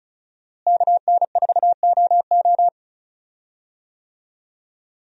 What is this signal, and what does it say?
Morse code, heard as a keyed tone, about 36 wpm, 700 Hz. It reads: KN4OO
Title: USA Call Signs - 5 Characters - Repeated 3x - 35wpm